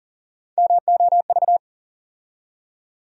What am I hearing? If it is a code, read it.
Morse code, heard as a keyed tone, about 39 wpm, 700 Hz. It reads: MOV
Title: USA Call Sign Postfixes - 3 Letters 40wpm